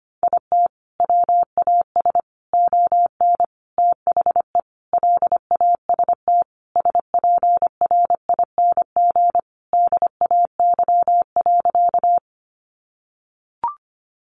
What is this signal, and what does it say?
Morse code, heard as a keyed tone, about 25 wpm, 700 Hz. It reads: IT WAS ON THE LAST SPRING DAY.
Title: Sentences from Top 1200 Words 25wpm